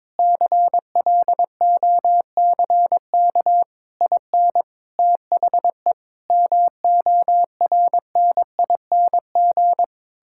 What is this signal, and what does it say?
Morse code, heard as a keyed tone, about 22 wpm, 700 Hz. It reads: CLOCK IN THE MORNING
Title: Pangram Challenge v2 - 22wpm